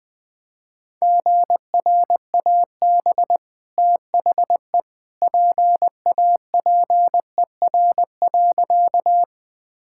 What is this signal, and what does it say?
Morse code, heard as a keyed tone, about 20 wpm, 700 Hz. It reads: GRAB THE PAPER.
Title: Easy Sentences with Hints - 20wpm